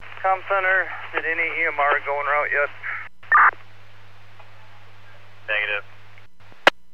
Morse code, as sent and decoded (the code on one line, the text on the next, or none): none